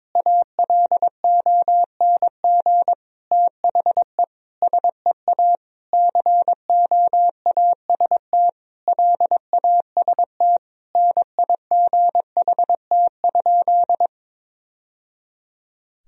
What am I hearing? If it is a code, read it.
Morse code, heard as a keyed tone, about 22 wpm, 700 Hz. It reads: ALONG THE SEA COAST LAST NIGHT?
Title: Sentences from Top 1000 Words - Review Part 4 - 22wpm